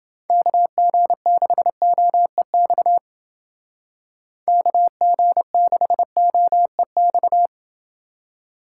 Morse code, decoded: KG6OEX KG6OEX

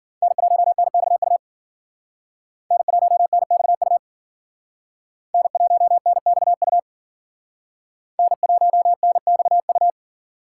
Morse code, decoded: N1NXU N1NXU N1NXU N1NXU

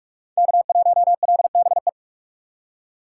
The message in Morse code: K1LBE

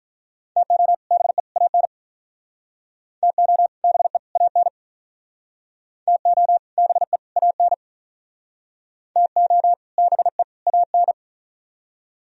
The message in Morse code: TO BE AN TO BE AN TO BE AN TO BE AN

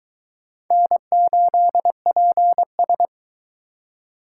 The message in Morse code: N8PS

